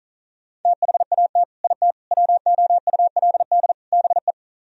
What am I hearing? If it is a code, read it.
Morse code, heard as a keyed tone, about 46 wpm, 700 Hz. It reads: THAT IT WOULD BE